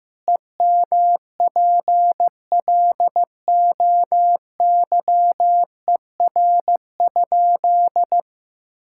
Morse code, heard as a keyed tone, about 15 wpm, 700 Hz. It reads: EMPLOYER?
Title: Sentences from Top 1600 Words 15wpm